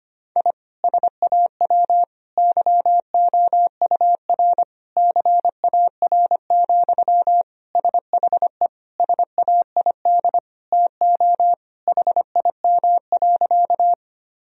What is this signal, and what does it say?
Morse code, heard as a keyed tone, about 25 wpm, 700 Hz. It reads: I SAW YOUR CAR, SHE SAID TO HIM.
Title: Sentences from Top 1000 Words - Review Part 1 - 25wpm